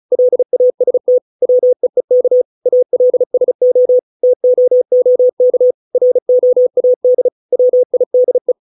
Morse code, heard as a keyed tone, about 34 wpm, 500 Hz. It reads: LAST WEEK ALSO TOOK ROAD WIDE